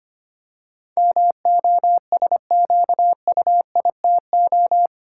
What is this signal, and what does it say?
Morse code, heard as a keyed tone, about 25 wpm, 700 Hz. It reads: MOSQUITO
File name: 8 Letter Words - First 3 Letters Given - 25wpm